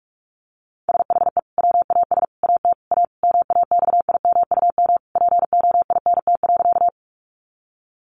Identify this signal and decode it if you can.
Morse code, heard as a keyed tone, about 44 wpm, 700 Hz. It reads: SHE WAS AT A MAXIMUM POINT.
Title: Sentences from Top 1300 Words 45wpm